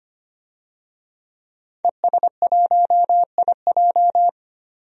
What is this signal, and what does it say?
Morse code, heard as a keyed tone, about 25 wpm, 700 Hz. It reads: ES1IJ
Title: DXCC-100 Call Signs - 5 Characters - 25wpm